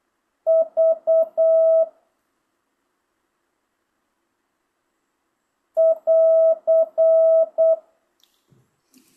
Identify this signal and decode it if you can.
Morse code, heard as a keyed tone, about 8 wpm, 650 Hz. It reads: V <AR>
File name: jh8cbh / 2025cw